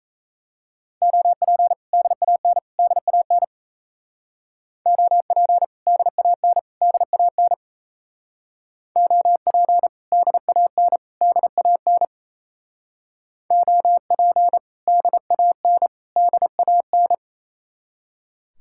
Morse code, heard as a keyed tone, about 34 wpm, 700 Hz. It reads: OP DAN DAN OP DAN DAN OP DAN DAN OP DAN DAN
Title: Speed-Racing Common QSO Phrases - 28wpm